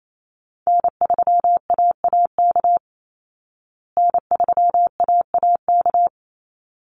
Morse code, heard as a keyed tone, about 28 wpm, 700 Hz. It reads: N3AAK N3AAK
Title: Call Signs Repeated 3x - 28wpm